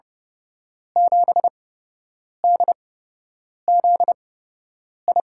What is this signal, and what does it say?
Morse code, heard as a keyed tone, about 31 wpm, spaced as for 8 wpm, 700 Hz. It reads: 7DZI
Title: USA Call Signs - 5 Characters - 8x Character Spacing - 30wpm